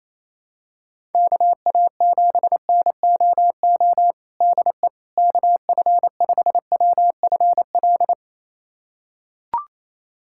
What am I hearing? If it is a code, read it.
Morse code, heard as a keyed tone, about 28 wpm, 700 Hz. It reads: KA7NOO DE KF5WFL
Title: USA Callsign DE Callsign 28wpm